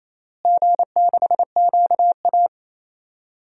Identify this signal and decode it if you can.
Morse code, heard as a keyed tone, about 28 wpm, 700 Hz. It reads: G6QA
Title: DXCC-100 Call Signs - 4 Characters - Repeated 3x - 28wpm